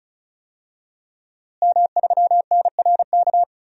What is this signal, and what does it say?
Morse code, heard as a keyed tone, about 35 wpm, 700 Hz. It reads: M3NRK
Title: DXCC-100 Call Signs - 5 Characters - Repeated 3x - 35wpm